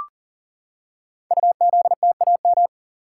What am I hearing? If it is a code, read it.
Morse code, heard as a keyed tone, about 39 wpm, 700 Hz. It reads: UZTAM
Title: Code Groups - 5 Letters - 40wpm